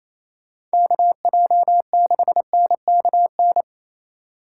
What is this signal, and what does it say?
Morse code, heard as a keyed tone, about 28 wpm, 700 Hz. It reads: KJ6NKN
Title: USA Call Signs - 6 Characters - 2x Character Spacing - 28wpm